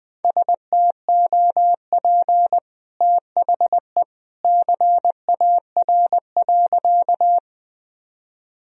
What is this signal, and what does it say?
Morse code, heard as a keyed tone, about 21 wpm, 700 Hz. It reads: STOP THE CAR.
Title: Sentences from Top 1000 Words - Review Part 1 - 20wpm